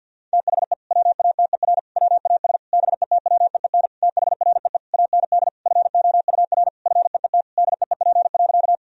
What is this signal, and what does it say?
Morse code, heard as a keyed tone, about 50 wpm, 700 Hz. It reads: THE WATER WAS BETWEEN THREE AND FOUR FEET DEEP.
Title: Sentences from Top 1000 Words - Review Part 3 - 50wpm